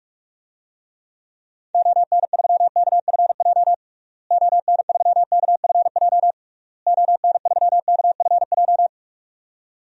Morse code, heard as a keyed tone, about 46 wpm, 700 Hz. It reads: ON3KFJ ON3KFJ ON3KFJ
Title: DXCC-100 Call Signs - 6 Characters - Repeated 3x - 45wpm